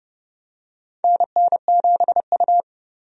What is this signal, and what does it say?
Morse code, heard as a keyed tone, about 32 wpm, 700 Hz. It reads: NN7U
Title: USA Call Signs - 4 Characters - 30wpm